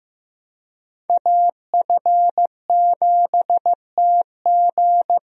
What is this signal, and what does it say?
Morse code, heard as a keyed tone, about 15 wpm, 700 Hz. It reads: AF7TG
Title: USA Call Signs - 5 Characters - Repeated 3x - 15wpm